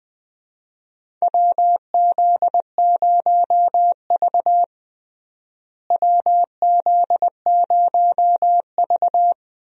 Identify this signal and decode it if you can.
Morse code, heard as a keyed tone, about 20 wpm, 700 Hz. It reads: WZ0V WZ0V
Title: USA Call Signs - 4 Characters - Repeated 3x - 20wpm